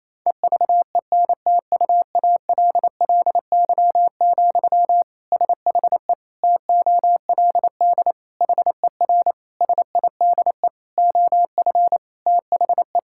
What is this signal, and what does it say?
Morse code, heard as a keyed tone, about 28 wpm, 700 Hz. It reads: EVENTUALLY, SHE TOLD HER SIDE OF THE
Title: Sentences from Top 1000 Words - Review Part 6 - 28wpm